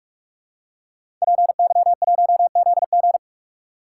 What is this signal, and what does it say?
Morse code, heard as a keyed tone, about 44 wpm, 700 Hz. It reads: PY1ZG